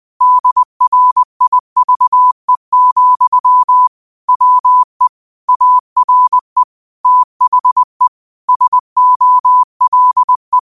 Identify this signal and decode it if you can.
Morse code, heard as a keyed tone, about 20 wpm, 1000 Hz. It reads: DRIVE, WE ARE THE SOLE
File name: qotdfiles / 20wpm